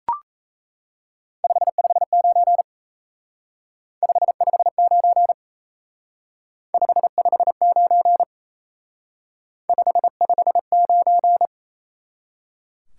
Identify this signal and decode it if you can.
Morse code, heard as a keyed tone, about 35 wpm, 700 Hz. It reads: 559 559 559 559
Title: Speed-Racing Top 180 QSO Elements 28wpm